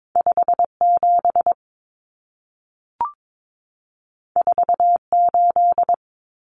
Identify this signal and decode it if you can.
Morse code, heard as a keyed tone, about 22 wpm, 700 Hz. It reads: 57 48
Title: Common Cut Numbers - 2-Digit 22wpm